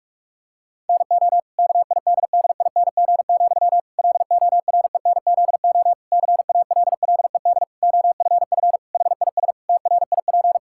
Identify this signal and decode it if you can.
Morse code, heard as a keyed tone, about 46 wpm, 700 Hz. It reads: NO KIDDING, LORENZO CALLED OFF HIS TRIP
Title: Pangram Challenge v2 - 45wpm